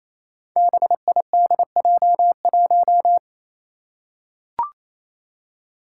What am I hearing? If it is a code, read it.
Morse code, heard as a keyed tone, about 29 wpm, 700 Hz. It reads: BIDJ1
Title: Five Challenging Letter-Number Combinations 28wpm